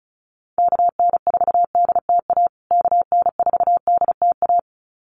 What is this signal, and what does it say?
Morse code, heard as a keyed tone, about 34 wpm, 700 Hz. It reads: KN4DTA KN4DTA
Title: USA Call Signs Repeated 2x 35wpm